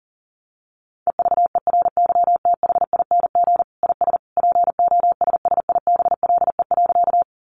Morse code, heard as a keyed tone, about 40 wpm, 700 Hz. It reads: EVERYTHING IS POSSIBLE.
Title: Sentences from Top 700 Words 40wpm